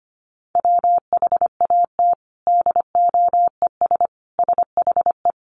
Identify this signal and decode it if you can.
Morse code, heard as a keyed tone, about 25 wpm, 700 Hz. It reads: WHAT DOES SHE
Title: Sentences from Top 1100 Words 25wpm